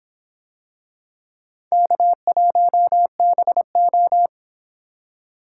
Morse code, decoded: K1BO

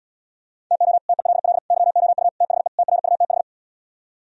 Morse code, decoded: TO TELL YOU THE TRUTH

